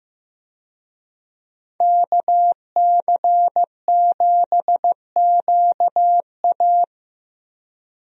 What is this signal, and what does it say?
Morse code, heard as a keyed tone, about 15 wpm, 700 Hz. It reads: KC7QA
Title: USA Call Signs - 5 Characters - 15wpm